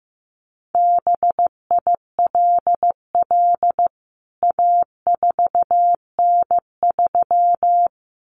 Morse code, heard as a keyed tone, about 14 wpm, 700 Hz. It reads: BILL A4N3
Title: CWT Exchange 15wpm